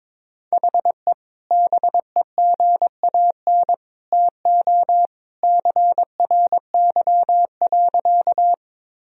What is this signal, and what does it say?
Morse code, heard as a keyed tone, about 21 wpm, 700 Hz. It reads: HE BEGAN TO CRY.
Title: Sentences from Top 1000 Words - Review Part 2 - 22wpm